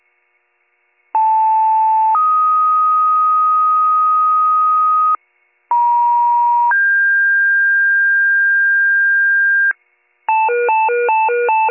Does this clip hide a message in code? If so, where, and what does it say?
Yes: two-tone page from 1.1 to 9.7 s, heard as a one-second tone, then a three-second tone, about 880 then 1250 Hz, about 950 then 1600 Hz, with noise.